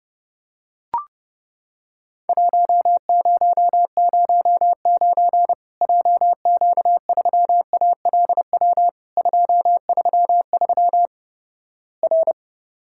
Morse code, heard as a keyed tone, about 32 wpm, 700 Hz. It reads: 1009 JQ3ALW 233 R
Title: Worked All Europe DX Contest - QTC Exchange - 30wpm